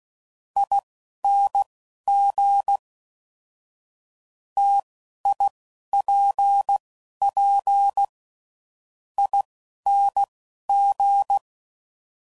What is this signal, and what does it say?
Morse code, heard as a keyed tone, about 16 wpm, spaced as for 10 wpm, 800 Hz. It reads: ING TIPPING
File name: Combinations / Difficult Combos / T AOEI's